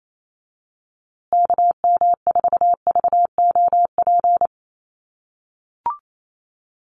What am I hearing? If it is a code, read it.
Morse code, heard as a keyed tone, about 28 wpm, 700 Hz. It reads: KM4VOP
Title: USA Call Signs Repeated 2x 28wpm